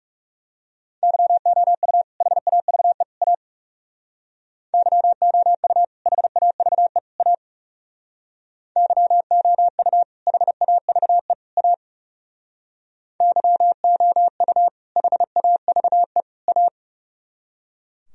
Morse code, heard as a keyed tone, about 38 wpm, 700 Hz. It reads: YOU HAVE A YOU HAVE A YOU HAVE A YOU HAVE A
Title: Speed-Racing - Sets of 3 Words 30wpm